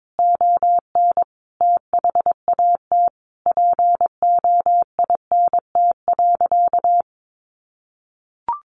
Morse code, heard as a keyed tone, about 22 wpm, 700 Hz. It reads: ON THAT POINT.
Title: Sentences from Top 1400 Words 22wpm